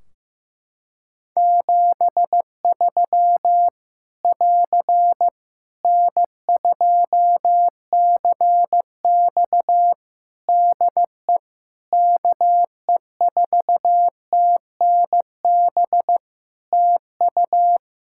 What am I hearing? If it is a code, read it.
Morse code, heard as a keyed tone, about 15 wpm, 700 Hz. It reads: 73 <AR> N2CX DE KE4TNB TU
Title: Practice QSO - Exchange 4 - Ending - 15wpm